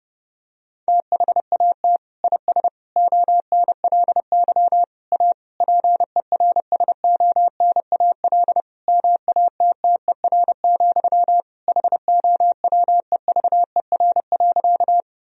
Morse code, heard as a keyed tone, about 30 wpm, 700 Hz. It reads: THAT IS ONLY A PERSONAL MATTER, HOWEVER.
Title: Sentences from Top 1000 Words - Review Part 4 - 30wpm